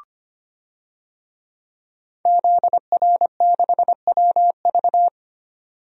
Morse code, heard as a keyed tone, about 25 wpm, 700 Hz. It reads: ZR6WV
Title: DXCC-100 Call Signs - 5 Characters - 25wpm